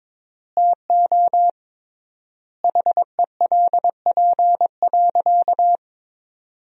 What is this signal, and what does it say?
Morse code, heard as a keyed tone, about 21 wpm, 700 Hz. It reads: TO HELP.
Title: 3x Word-Spacing - Sentences from Top 300 Words - 22wpm